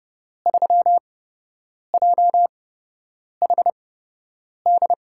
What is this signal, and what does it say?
Morse code, heard as a keyed tone, about 29 wpm, spaced as for 8 wpm, 700 Hz. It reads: 3JHD
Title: USA Call Signs - 6 Characters - 8x Character Spacing - 30wpm